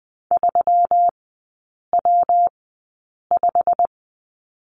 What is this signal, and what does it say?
Morse code, heard as a keyed tone, about 20 wpm, spaced as for 8 wpm, 700 Hz. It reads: 3W5
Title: 159 Lesson 26e - Letter g - Three Characters 20wpm